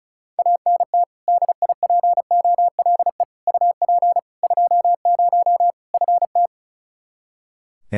Morse code, heard as a keyed tone, about 34 wpm, 700 Hz. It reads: ANT DIPOLE UP 20 FT
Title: Practice QSO - Exchange 2 - Info - 35wpm